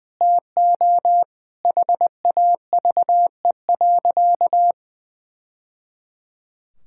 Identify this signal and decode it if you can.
Morse code, heard as a keyed tone, about 21 wpm, 700 Hz. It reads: TO HAVE.
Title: Sentences from Top 1000 Words - Review Part 6 - 20wpm